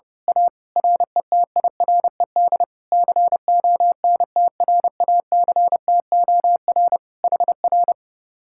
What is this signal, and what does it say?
Morse code, heard as a keyed tone, about 29 wpm, 700 Hz. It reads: A RETIRED CONTRACTOR HR
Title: Speed-Racing Common QSO Phrases - 30wpm